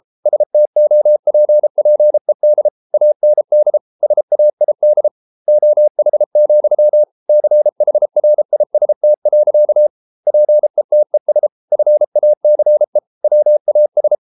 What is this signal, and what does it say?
Morse code, heard as a keyed tone, about 33 wpm, 600 Hz. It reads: STOPPED AND SAID OH, CHRIST. PETES FACE WAS